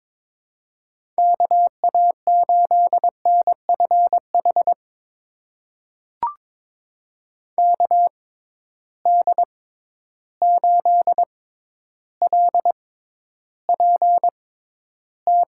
Morse code, heard as a keyed tone, about 22 wpm, spaced as for 11 wpm, 700 Hz. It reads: KA8NFH KD8LPT
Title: USA Call Signs - 6 Characters - 6x Character Spacing - 22wpm